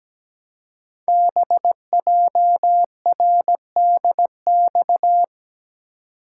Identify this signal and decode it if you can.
Morse code, heard as a keyed tone, about 17 wpm, 700 Hz. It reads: BJRDX
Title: Code Groups - 5 Letters - 17wpm